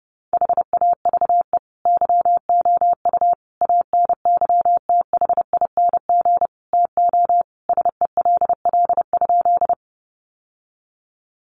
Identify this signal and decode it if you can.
Morse code, heard as a keyed tone, about 30 wpm, 700 Hz. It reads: HAVE YOU ANYTHING TO SELL?
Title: Sentences from Top 1100 Words 30wpm